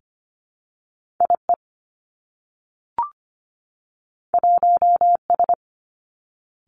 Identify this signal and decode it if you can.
Morse code, heard as a keyed tone, about 25 wpm, 700 Hz. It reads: IE 1S
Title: Two Challenging Letter-Number Combinations 25wpm